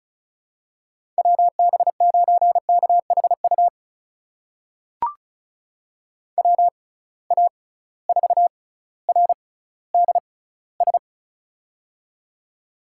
Morse code, heard as a keyed tone, about 36 wpm, spaced as for 18 wpm, 700 Hz. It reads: WB9KHU WA4RDS
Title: USA Call Signs - 6 Characters - 6x Character Spacing - 35wpm